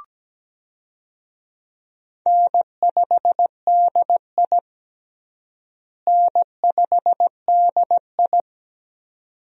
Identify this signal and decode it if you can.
Morse code, heard as a keyed tone, about 17 wpm, 700 Hz. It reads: N5DI N5DI
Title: USA Call Signs - 4 Characters - Repeated 2x - 17wpm